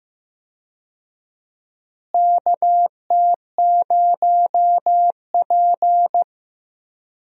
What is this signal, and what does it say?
Morse code, heard as a keyed tone, about 15 wpm, 700 Hz. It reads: KT0P
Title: USA Call Signs - 4 Characters - Repeated 3x - 15wpm